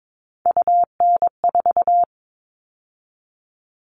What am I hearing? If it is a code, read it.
Morse code, heard as a keyed tone, about 22 wpm, 700 Hz. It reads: UN4
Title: All Cut Numbers - 3-Digit 22wpm